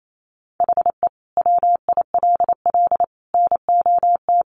Morse code, heard as a keyed tone, about 28 wpm, 700 Hz. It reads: HE WILL NOT
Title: Sentences from Top 600 Words 28wpm